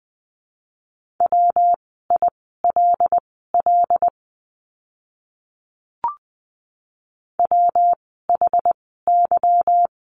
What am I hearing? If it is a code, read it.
Morse code, heard as a keyed tone, about 20 wpm, spaced as for 15 wpm, 700 Hz. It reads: WILL WHY
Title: 237 Lesson 38f - Letter q - Words 20wpm